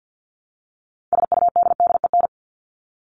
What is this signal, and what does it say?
Morse code, heard as a keyed tone, about 50 wpm, 700 Hz. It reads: SUDDEN
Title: LY Suffix - Top 100 Words with Base Words 50wpm